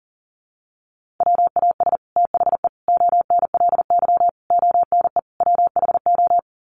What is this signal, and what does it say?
Morse code, heard as a keyed tone, about 38 wpm, 700 Hz. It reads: WAS THE ONLY ONE WHO
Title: Sets of 5 Words Encore2 40wpm